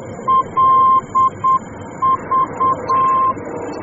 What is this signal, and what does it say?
Morse code, heard as a keyed tone, about 8 wpm, 1050 Hz, with noise. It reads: LV
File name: germany / mx / LV-D